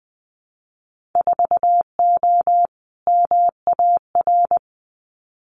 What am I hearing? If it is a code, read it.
Morse code, heard as a keyed tone, about 20 wpm, 700 Hz. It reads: 4O MAR
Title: Winter Field Day Exchanges Encore 20wpm